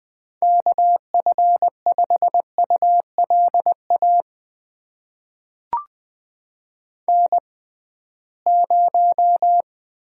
Morse code, heard as a keyed tone, about 20 wpm, spaced as for 15 wpm, 700 Hz. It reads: KF5ULA N0
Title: USA Call Signs - 6x Character Spacing - 20wpm